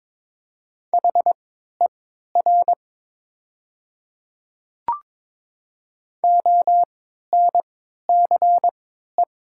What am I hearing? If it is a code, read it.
Morse code, heard as a keyed tone, about 22 wpm, spaced as for 12 wpm, 700 Hz. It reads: HER ONCE
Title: 3x Character Spacing - Top 300 Words - 22wpm